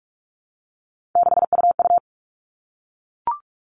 Morse code, heard as a keyed tone, about 44 wpm, 700 Hz. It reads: 6UU
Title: All Cut Numbers - 3-Digit 45wpm